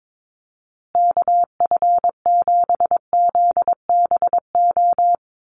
Morse code, heard as a keyed tone, about 22 wpm, 700 Hz. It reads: KF7ZBO
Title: USA Call Signs Repeated 3x 22wpm